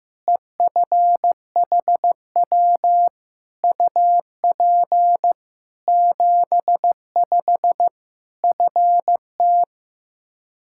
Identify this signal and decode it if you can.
Morse code, heard as a keyed tone, about 15 wpm, 700 Hz. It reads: EFHW UP 75 FT
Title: Practice QSO - Exchange 2 - Info - 15wpm